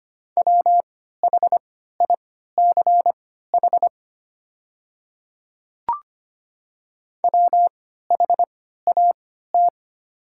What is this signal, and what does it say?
Morse code, heard as a keyed tone, about 25 wpm, spaced as for 14 wpm, 700 Hz. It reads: WHICH WHAT
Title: 3x Character Spacing - Top 100 Words - 25wpm